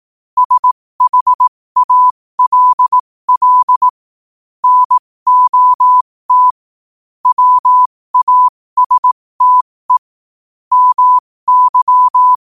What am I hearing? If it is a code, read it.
Morse code, heard as a keyed tone, about 18 wpm, spaced as for 15 wpm, 1000 Hz. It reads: SHALL NOT WASTE MY
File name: qotdfiles / 15wpm